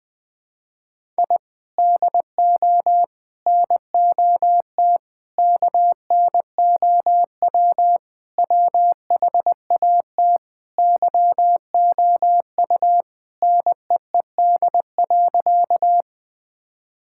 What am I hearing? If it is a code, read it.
Morse code, heard as a keyed tone, about 20 wpm, 700 Hz. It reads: I DO NOT KNOW WHAT YOU NEED.
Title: Sentences with 5-Word Phrases - 20wpm